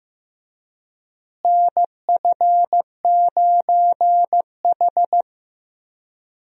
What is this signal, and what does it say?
Morse code, heard as a keyed tone, about 15 wpm, 700 Hz. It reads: NF9H
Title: USA Call Signs - 4 Characters - Repeated 2x - 15wpm